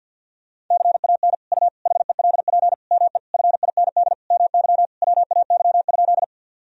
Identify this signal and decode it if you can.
Morse code, heard as a keyed tone, about 52 wpm, 700 Hz. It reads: CAN U HELP ME FIND MY WAY?
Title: Sentences from Top 200 Words and Abbreviations 50wpm